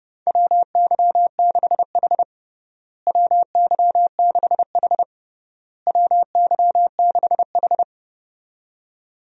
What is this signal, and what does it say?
Morse code, heard as a keyed tone, about 30 wpm, 700 Hz. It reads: WY6H WY6H WY6H
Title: USA Call Signs - 4 Characters - Repeated 3x - 30wpm